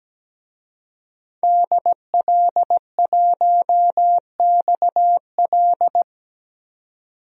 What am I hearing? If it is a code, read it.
Morse code, heard as a keyed tone, about 17 wpm, 700 Hz. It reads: DL1XL
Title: DXCC-100 Call Signs - 5 Characters - 17wpm